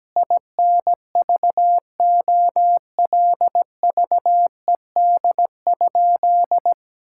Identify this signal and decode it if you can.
Morse code, heard as a keyed tone, about 17 wpm, 700 Hz. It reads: INVOLVED?
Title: Sentences from Top 1000 Words - Review Part 5 - 17wpm